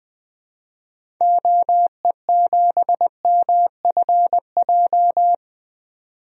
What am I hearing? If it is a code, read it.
Morse code, heard as a keyed tone, about 20 wpm, 700 Hz. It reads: OE7MFJ
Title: DXCC-100 Call Signs - 6 Characters - 20wpm